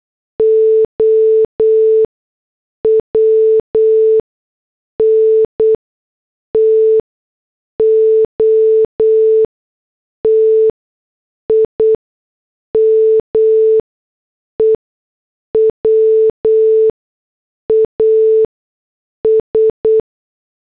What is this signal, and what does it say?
Morse code, heard as a keyed tone, about 8 wpm, spaced as for 6 wpm, 440 Hz. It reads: OWN TO TIME WAS